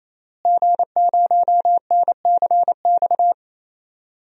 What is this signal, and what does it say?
Morse code, heard as a keyed tone, about 28 wpm, 700 Hz. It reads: G0NCX
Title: DXCC-100 Call Signs - 5 Characters - Repeated 3x - 28wpm